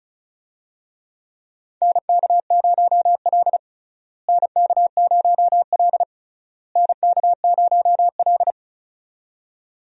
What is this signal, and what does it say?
Morse code, heard as a keyed tone, about 34 wpm, 700 Hz. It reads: NK0L NK0L NK0L
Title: USA Call Signs - 4 Characters - Repeated 3x - 35wpm